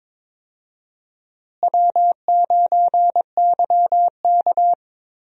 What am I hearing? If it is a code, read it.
Morse code, heard as a keyed tone, about 21 wpm, 700 Hz. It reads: W9YK
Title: USA Call Signs - 4 Characters - Repeated 3x - 22wpm